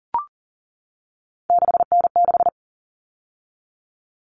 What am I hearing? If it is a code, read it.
Morse code, heard as a keyed tone, about 40 wpm, 700 Hz. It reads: 6N6
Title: All Cut Numbers - 3-Digit 40wpm